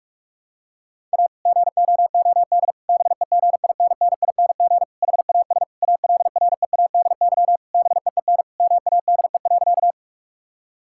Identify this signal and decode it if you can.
Morse code, heard as a keyed tone, about 46 wpm, 700 Hz. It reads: A GOOD BEGINNING HAS ALREADY BEEN MADE.
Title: Sentences from Top 1000 Words - Review Part 5 - 45wpm